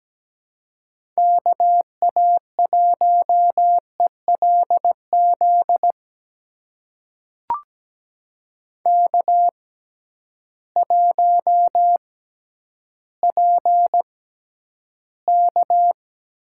Morse code, heard as a keyed tone, about 17 wpm, spaced as for 10 wpm, 700 Hz. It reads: KA1ELZ K1PK
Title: USA Call Signs - 6x Character Spacing - 17wpm